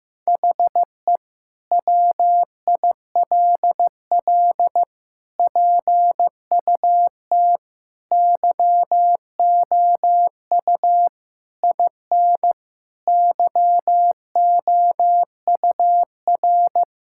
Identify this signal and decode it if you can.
Morse code, heard as a keyed tone, about 15 wpm, 700 Hz. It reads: HE WILL PUT YOU IN YOUR
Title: Sentences from Top 1000 Words - Review Part 5 - 15wpm